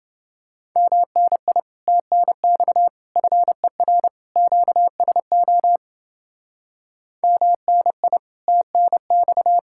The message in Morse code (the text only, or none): MNI TNX FER QSO MNI TNX